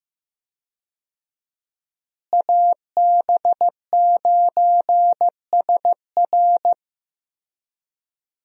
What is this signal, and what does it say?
Morse code, heard as a keyed tone, about 15 wpm, 700 Hz. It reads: AB9SR
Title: USA Call Signs - 5 Characters - Repeated 2x - 15wpm